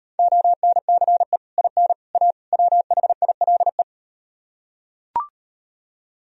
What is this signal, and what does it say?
Morse code, heard as a keyed tone, about 38 wpm, 700 Hz. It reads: ONCE IN A WHILE